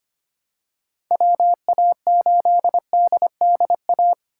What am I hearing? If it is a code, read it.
Morse code, heard as a keyed tone, about 25 wpm, 700 Hz. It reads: WA8DDA
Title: USA Call Signs - 8x Character Spacing - 25wpm